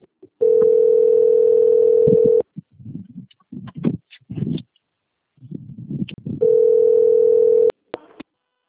Ringback tone (North America), 440 + 480 Hz.